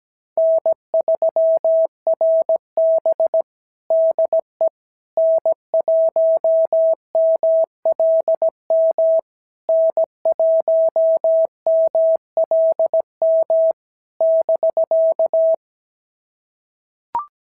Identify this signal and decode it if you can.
Morse code, heard as a keyed tone, about 16 wpm, 650 Hz. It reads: N3RB DE N1MLM N1MLM <BK>
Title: SOTA QSO Practice 17wpm